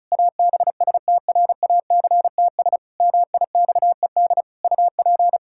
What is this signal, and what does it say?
Morse code, heard as a keyed tone, about 36 wpm, 700 Hz. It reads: ABSTRACTS MIXED UP
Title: Pangram Challenge v2 - 35wpm